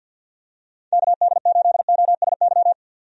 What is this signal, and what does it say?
Morse code, heard as a keyed tone, about 53 wpm, 700 Hz. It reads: KD8OSY
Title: USA Call Signs - 6 Characters - 50wpm